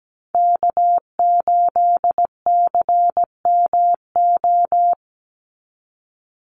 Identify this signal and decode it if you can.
Morse code, heard as a keyed tone, about 17 wpm, 700 Hz. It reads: K8CMO
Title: Call Signs Repeated 3x - 17wpm